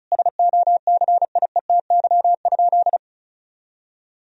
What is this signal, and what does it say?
Morse code, heard as a keyed tone, about 36 wpm, 700 Hz. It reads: SOCIETY?
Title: Sentences from Top 1000 Words - Review Part 4 - 35wpm